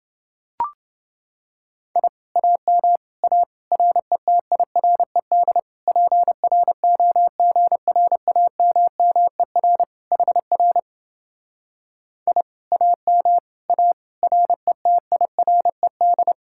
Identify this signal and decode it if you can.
Morse code, heard as a keyed tone, about 29 wpm, 700 Hz. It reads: I AM A RETIRED PROGRAMMER HR I AM A RETIRED